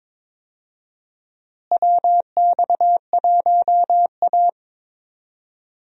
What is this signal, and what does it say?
Morse code, heard as a keyed tone, about 22 wpm, 700 Hz. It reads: WX1A
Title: USA Call Signs - 4 Characters - Repeated 2x - 22wpm